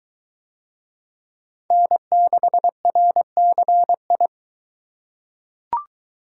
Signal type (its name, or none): Morse code